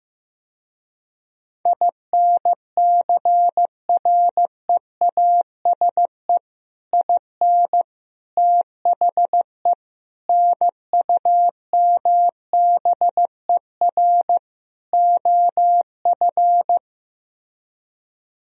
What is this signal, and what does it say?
Morse code, heard as a keyed tone, about 15 wpm, 700 Hz. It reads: INCREASE IN THE NUMBER OF
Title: Speed-Racing - Sets of 5 Words 15wpm